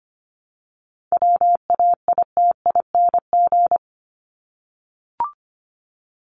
Morse code, heard as a keyed tone, about 25 wpm, 700 Hz. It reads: WAITING
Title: ING Suffix - Top 100 Words with Base Words 25wpm